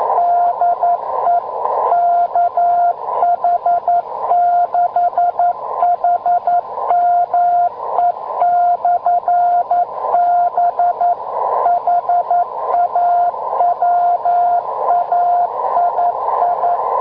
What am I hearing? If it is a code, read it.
Morse code, heard as a keyed tone, about 11 wpm, 700 Hz, with radio noise. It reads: DE KH6HME/B HAWAII